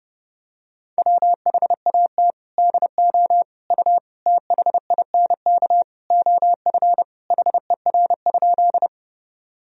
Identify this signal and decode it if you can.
Morse code, heard as a keyed tone, about 32 wpm, 700 Hz. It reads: WHAT DO U THINK OF HER?
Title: Sentences from Top 200 Words and Abbreviations 30wpm